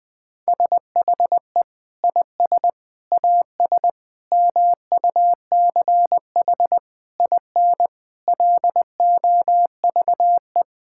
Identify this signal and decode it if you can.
Morse code, heard as a keyed tone, about 20 wpm, 700 Hz. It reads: SHE IS AS MUCH IN LOVE
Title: Sentences from Top 1000 Words - Review Part 5 - 20wpm